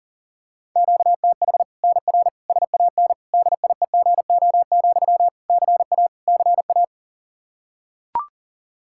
Morse code, heard as a keyed tone, about 40 wpm, 700 Hz. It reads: QTH NR SAN DIEGO, CA CA
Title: Practice QSO - Exchange 1 - Beginning - 40wpm